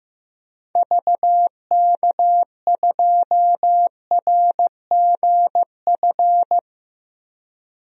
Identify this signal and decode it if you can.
Morse code, heard as a keyed tone, about 15 wpm, 700 Hz. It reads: VK2RGF
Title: DXCC-100 Call Signs - 6 Characters - 15wpm